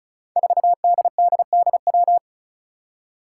Morse code, heard as a keyed tone, about 36 wpm, 700 Hz. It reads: 4DDDW
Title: Five Challenging Letter-Number Combinations 35wpm